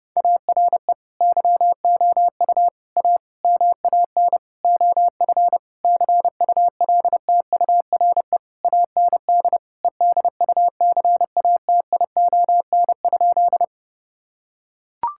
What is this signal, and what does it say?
Morse code, heard as a keyed tone, about 32 wpm, 700 Hz. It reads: ARE YOU A MAN OF CULTURE AND EDUCATION?
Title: Sentences from Top 1000 Words - Review Part 6 - 30wpm